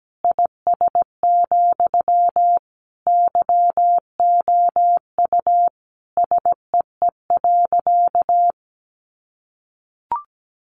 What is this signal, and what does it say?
Morse code, heard as a keyed tone, about 17 wpm, 700 Hz. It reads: IS, YOU SEE.
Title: Sentences from Top 100 Words 17wpm